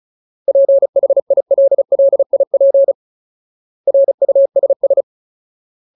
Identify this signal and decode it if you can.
Morse code, heard as a keyed tone, about 35 wpm, 550 Hz. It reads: PHILLIP RUSS
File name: Ultimate Word List / 35WPM Ultimate